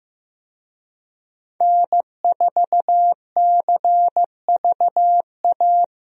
Morse code, heard as a keyed tone, about 16 wpm, 700 Hz. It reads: N4CVA
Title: USA Call Signs - 5 Characters - Repeated 3x - 15wpm